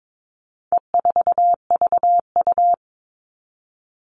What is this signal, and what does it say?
Morse code, heard as a keyed tone, about 22 wpm, 700 Hz. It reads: E4VU
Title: All Cut Numbers - 4-Digit 22wpm